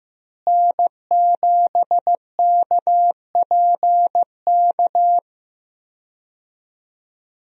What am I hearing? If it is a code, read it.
Morse code, heard as a keyed tone, about 15 wpm, 700 Hz. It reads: N7KPK
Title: USA Call Signs - 5 Characters - Repeated 2x - 15wpm